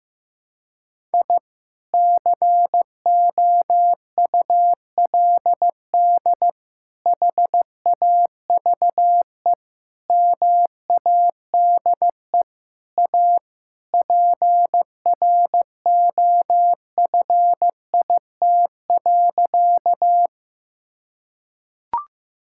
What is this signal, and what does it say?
Morse code, heard as a keyed tone, about 16 wpm, 700 Hz. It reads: I COULD HAVE MADE A PROFIT.
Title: Sentences from Top 1000 Words - Review Part 5 - 15wpm